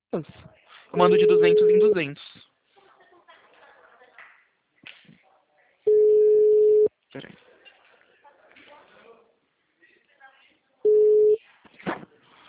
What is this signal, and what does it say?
Ringback tone (Europe, 425 Hz), with speech.
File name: out- -2030- - - .12144